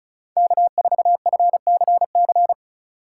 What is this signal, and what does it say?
Morse code, heard as a keyed tone, about 34 wpm, 700 Hz. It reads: K4FCC
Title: K1USN SST QSO Practice - 35wpm